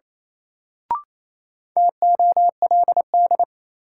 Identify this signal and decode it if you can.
Morse code, heard as a keyed tone, about 28 wpm, 700 Hz. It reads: TOLD
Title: Sentences from Top 400 Words - New Word Drill - 28wpm